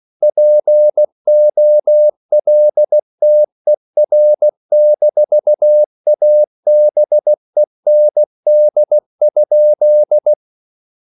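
Morse code, heard as a keyed tone, about 16 wpm, 600 Hz. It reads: POLTER-ABEND?
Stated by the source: Slow (QRS) Morse code at 16 wpm, from CW med Gnister.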